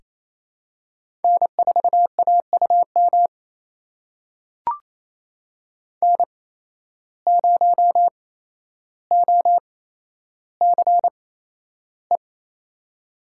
Morse code, heard as a keyed tone, about 28 wpm, spaced as for 12 wpm, 700 Hz. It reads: N4AUM N0OCE
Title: USA Call Signs - 5 Characters - 8x Character Spacing - 28wpm